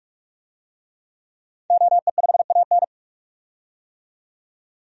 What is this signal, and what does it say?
Morse code, heard as a keyed tone, about 41 wpm, 700 Hz. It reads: OE5AN
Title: DXCC-100 Call Signs - 5 Characters - 45wpm